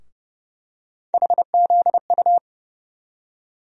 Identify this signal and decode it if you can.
Morse code, heard as a keyed tone, about 30 wpm, 700 Hz. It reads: HZU
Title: USA Call Sign Postfixes - 3 Letters 30wpm